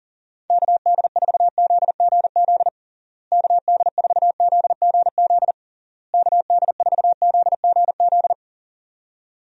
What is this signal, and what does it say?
Morse code, heard as a keyed tone, about 40 wpm, 700 Hz. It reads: KD4ZGZ KD4ZGZ KD4ZGZ
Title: USA Call Signs - 6 Characters - Repeated 3x - 40wpm